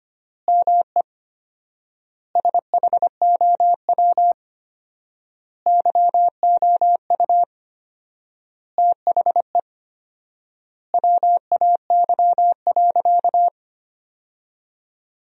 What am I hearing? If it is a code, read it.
Morse code, heard as a keyed tone, about 25 wpm, 700 Hz. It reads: ME SHOW YOU THE WAY.
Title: 4x Word-Spacing - Sentences from Top 300 Words - 25wpm